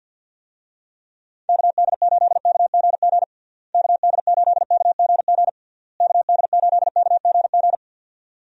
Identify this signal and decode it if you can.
Morse code, heard as a keyed tone, about 52 wpm, 700 Hz. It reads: KD8KGG KD8KGG KD8KGG
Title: USA Call Signs - 6 Characters - Repeated 3x - 50wpm